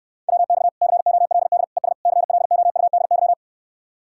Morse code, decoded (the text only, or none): WX COLD ES CLOUDY